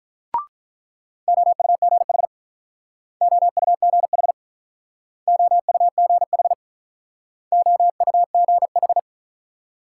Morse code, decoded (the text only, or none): OUGH OUGH OUGH OUGH